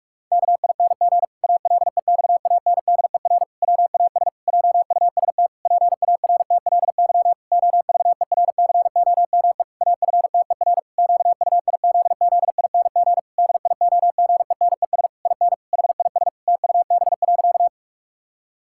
Morse code, heard as a keyed tone, about 46 wpm, 700 Hz. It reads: KING ALEXANDER WAS JUST PARTLY OVERCOME AFTER QUIZZING DIOGENES IN HIS TUB.
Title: Pangram Challenge v2 - 45wpm